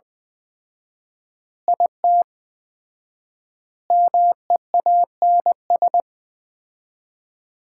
Morse code, heard as a keyed tone, about 21 wpm, 700 Hz. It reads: IT MEANS
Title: 4x Word-Spacing - Sets of 5 Words - 20wpm